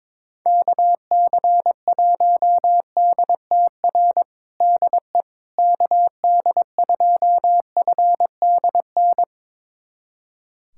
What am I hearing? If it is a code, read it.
Morse code, heard as a keyed tone, about 22 wpm, 700 Hz. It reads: KC1DTR DE KD2FDN
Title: USA Callsign DE Callsign 22wpm